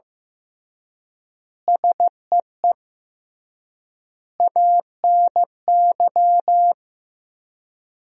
Morse code, heard as a keyed tone, about 15 wpm, 700 Hz. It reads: SEE ANY
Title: 3x Word-Spacing - Sentences from Top 500 Words - 15wpm